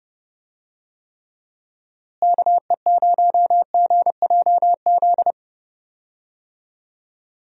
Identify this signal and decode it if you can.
Morse code, heard as a keyed tone, about 30 wpm, 700 Hz. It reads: KE0GJZ
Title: USA Call Signs - 6 Characters - Repeated 2x - 30wpm